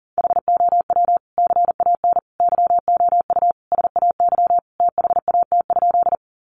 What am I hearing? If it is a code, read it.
Morse code, heard as a keyed tone, about 40 wpm, 700 Hz. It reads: HOW CAN YOU SAY THAT?
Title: Sentences from Top 200 Words 40wpm